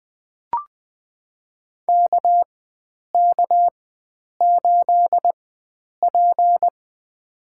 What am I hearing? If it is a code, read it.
Morse code, heard as a keyed tone, about 20 wpm, spaced as for 9 wpm, 700 Hz. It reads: KK8P
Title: USA Call Signs - 4 Characters - 4x Character Spacing - 20wpm